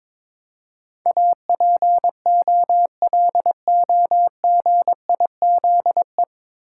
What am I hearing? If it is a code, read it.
Morse code, heard as a keyed tone, about 22 wpm, 700 Hz. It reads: APOLOGIZE
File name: 9 Letter Words - First 3 Letters Given - 22wpm